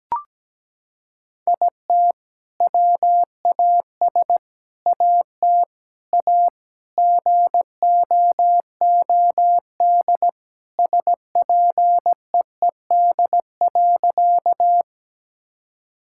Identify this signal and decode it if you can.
Morse code, heard as a keyed tone, about 17 wpm, 700 Hz. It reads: IT WAS AT A GOOD SPEED.
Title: Sentences from Top 1000 Words - Review Part 5 - 17wpm